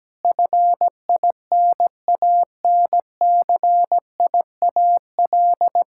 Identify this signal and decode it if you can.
Morse code, heard as a keyed tone, about 17 wpm, 700 Hz. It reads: FINANCIAL
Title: Sentences from Top 1000 Words - Review Part 6 - 17wpm